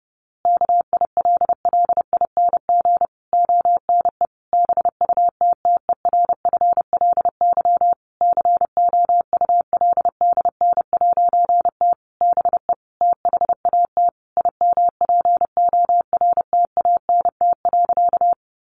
Morse code, heard as a keyed tone, about 30 wpm, 700 Hz. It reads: KILLING ONE BUTTERFLY COULDN'T BE THAT IMPORTANT.
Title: A Sound Of Thunder 30wpm